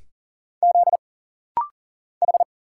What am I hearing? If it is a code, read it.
Morse code, heard as a keyed tone, about 40 wpm, 700 Hz. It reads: Z H
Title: Single Letter-Number Alt - Rapid-Fire 40wpm